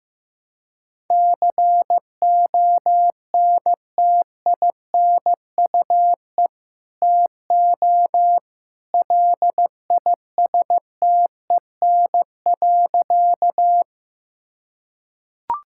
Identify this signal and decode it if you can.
Morse code, heard as a keyed tone, about 15 wpm, 700 Hz. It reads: CONTINUE TO LISTEN.
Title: Sentences from Top 1000 Words - Review Part 6 - 15wpm